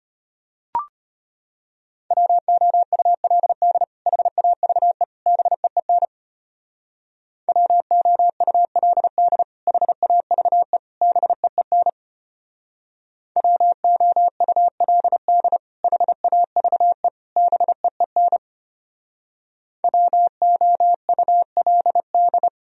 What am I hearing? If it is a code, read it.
Morse code, heard as a keyed tone, about 32 wpm, 700 Hz. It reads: WOULD HAVE BEEN WOULD HAVE BEEN WOULD HAVE BEEN WOULD